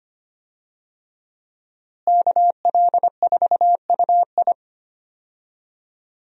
Morse code, heard as a keyed tone, about 25 wpm, 700 Hz. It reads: KL4UI
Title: USA Call Signs - 5 Characters - Repeated 3x - 25wpm